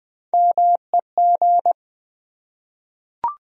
Morse code, heard as a keyed tone, about 20 wpm, 700 Hz. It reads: MEG
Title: USA Common Names - Max 4 Letters - 20wpm